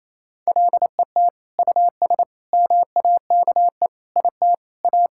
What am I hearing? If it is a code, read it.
Morse code, heard as a keyed tone, about 27 wpm, 700 Hz. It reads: LET US MAKE IT A
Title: Sentences from Top 1600 Words 28wpm